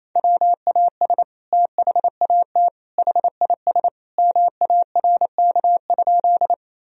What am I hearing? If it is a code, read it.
Morse code, heard as a keyed tone, about 28 wpm, 700 Hz. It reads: WAS THAT HIS MARK?
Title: Sentences from Top 1000 Words - Review Part 1 - 28wpm